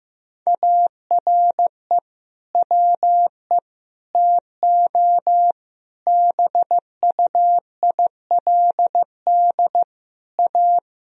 Morse code, heard as a keyed tone, about 15 wpm, 700 Hz. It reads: ARE WE TO BUILD A
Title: Sentences from Top 1000 Words - Review Part 1 - 15wpm